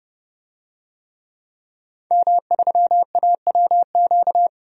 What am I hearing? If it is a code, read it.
Morse code, heard as a keyed tone, about 30 wpm, 700 Hz. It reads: M3AWQ
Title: DXCC-100 Call Signs - 5 Characters - 30wpm